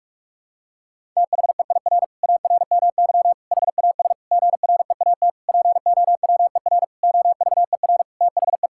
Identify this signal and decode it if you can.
Morse code, heard as a keyed tone, about 47 wpm, 700 Hz. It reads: THEIR ARMY HAS GREAT POWER OVER THE
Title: Sentences from Top 1000 Words - Review Part 5 - 45wpm